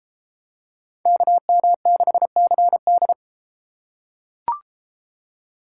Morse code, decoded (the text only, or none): KM6CD